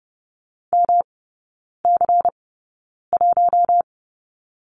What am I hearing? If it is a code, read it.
Morse code, heard as a keyed tone, about 30 wpm, spaced as for 9 wpm, 700 Hz. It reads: MC1
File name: Lesson 18e - Letter m - Three Characters 30wpm